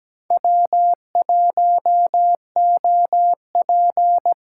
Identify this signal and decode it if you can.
Morse code, heard as a keyed tone, about 20 wpm, 700 Hz. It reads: W1OP
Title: Speed Racing - USA 1x2 Call Signs 17wpm